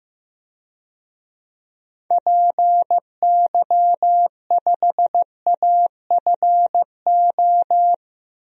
Morse code, heard as a keyed tone, about 16 wpm, 700 Hz. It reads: PY5AFO
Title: DXCC-100 Call Signs - 6 Characters - 15wpm